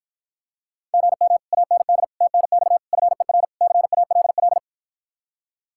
Morse code, heard as a keyed tone, about 53 wpm, 700 Hz. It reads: GM AND TNX FER CALL